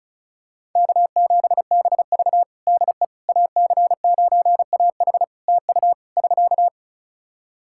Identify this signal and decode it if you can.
Morse code, heard as a keyed tone, about 36 wpm, 700 Hz. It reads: K7BV DE AC9AH TU <SK>
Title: Practice QSO - Exchange 4 - Ending - 35wpm